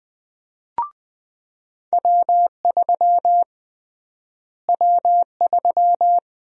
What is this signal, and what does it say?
Morse code, heard as a keyed tone, about 20 wpm, 700 Hz. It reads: W3 W3
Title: USA Call Sign Prefixes - Repeated 2x - 20wpm